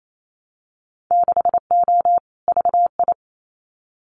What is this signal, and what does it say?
Morse code, heard as a keyed tone, about 28 wpm, 690 Hz. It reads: 6O VI
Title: Winter Field Day Exchanges Encore 28wpm